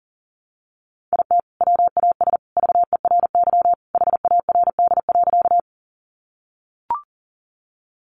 Morse code, heard as a keyed tone, about 40 wpm, 700 Hz. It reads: IT WAS VERY HARD.
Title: Sentences from Top 300 Words 40wpm